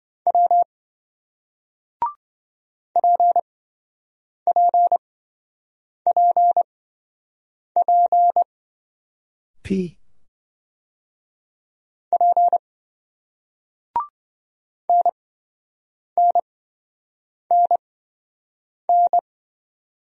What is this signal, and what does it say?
Morse code, heard as a keyed tone, about 27 wpm, spaced as for 7 wpm, 700 Hz. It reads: W PPPP P NNNN